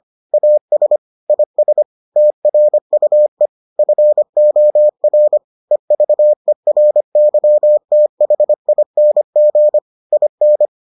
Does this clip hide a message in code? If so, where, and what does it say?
Yes: Morse code from 0.3 to 10.7 s, heard as a keyed tone, about 25 wpm, 600 Hz. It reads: AS IS TRUE FOR EVERYTHING IN